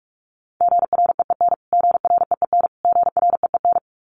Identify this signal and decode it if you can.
Morse code, heard as a keyed tone, about 44 wpm, 700 Hz. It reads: GREEN GREEN GREEN
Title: Top 500 Words Repeated 45wpm